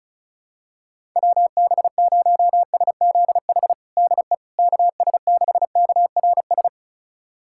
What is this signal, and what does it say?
Morse code, heard as a keyed tone, about 34 wpm, 700 Hz. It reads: WB0SZH DE KS6KRS
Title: USA Callsign DE Callsign 35wpm